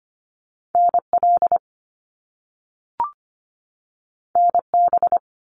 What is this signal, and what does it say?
Morse code, heard as a keyed tone, about 25 wpm, 700 Hz. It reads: NL NB
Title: Canadian Provinces and Territories 25wpm